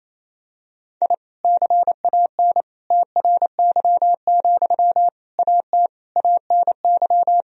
Morse code, heard as a keyed tone, about 27 wpm, 700 Hz. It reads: I CAN TRY, AT ANY
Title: Sentences from Top 1000 Words - Review Part 4 - 28wpm